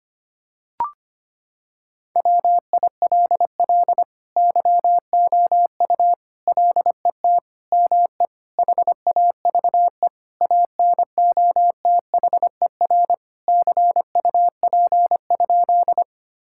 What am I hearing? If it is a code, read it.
Morse code, heard as a keyed tone, about 25 wpm, 700 Hz. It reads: WILL YOU LET ME HAVE ANOTHER CUP?
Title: Sentences from Top 1600 Words 25wpm